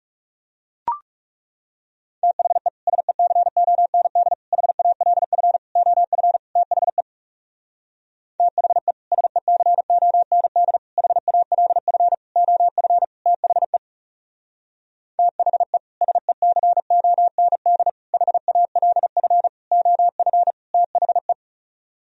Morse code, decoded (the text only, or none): THE SECOND HALF OF THE THE SECOND HALF OF THE THE SECOND HALF OF THE